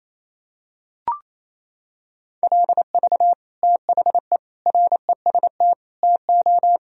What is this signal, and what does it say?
Morse code, heard as a keyed tone, about 28 wpm, 700 Hz. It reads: LV THE REST TO
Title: Sentences from Top 500 Words and Abbreviations 28wpm